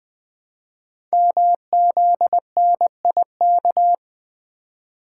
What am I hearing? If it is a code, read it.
Morse code, heard as a keyed tone, about 20 wpm, 700 Hz. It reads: MZNIK
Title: Code Groups - 5 Letters - 20wpm